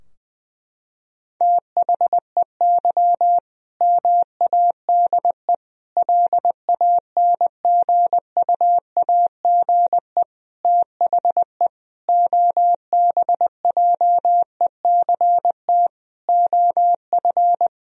Morse code, heard as a keyed tone, about 21 wpm, 700 Hz. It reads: THEY MADE LANGUAGE THE OBJECT OF
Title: Sentences from Top 1000 Words - Review Part 3 - 20wpm